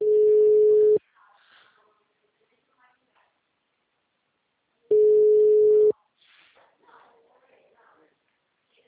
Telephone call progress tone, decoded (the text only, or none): ringback tone